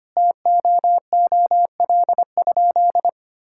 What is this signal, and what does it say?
Morse code, heard as a keyed tone, about 25 wpm, 700 Hz. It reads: TOOL?
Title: Sentences from Top 1000 Words - Review Part 6 - 25wpm